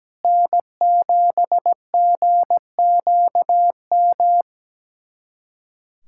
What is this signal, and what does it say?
Morse code, heard as a keyed tone, about 17 wpm, 700 Hz. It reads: N7GQM
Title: USA Callsign DE Callsign 17wpm